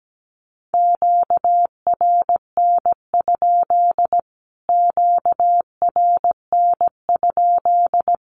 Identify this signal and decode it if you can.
Morse code, heard as a keyed tone, about 17 wpm, 700 Hz. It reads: QRN? QRN?
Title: Q-Codes Repeated 3x 17wpm